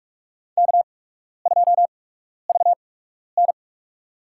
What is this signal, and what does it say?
Morse code, heard as a keyed tone, about 46 wpm, spaced as for 12 wpm, 700 Hz. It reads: K2VN
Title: USA Call Signs - 4 Characters - 8x Character Spacing - 45wpm